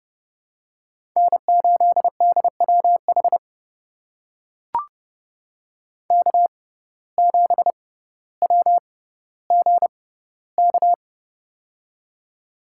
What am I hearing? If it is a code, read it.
Morse code, heard as a keyed tone, about 30 wpm, spaced as for 15 wpm, 700 Hz. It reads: N8DWH K7WGK
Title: USA Call Signs - 5 Characters - 6x Character Spacing - 30wpm